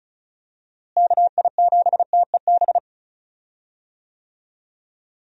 Morse code, heard as a keyed tone, about 36 wpm, 700 Hz. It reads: KI7TEB
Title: USA Call Signs - 6 Characters - Repeated 3x - 35wpm